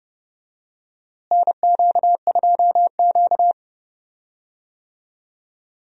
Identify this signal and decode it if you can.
Morse code, heard as a keyed tone, about 30 wpm, 700 Hz. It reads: NQ2Q
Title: USA Call Signs - 4 Characters - Repeated 3x - 30wpm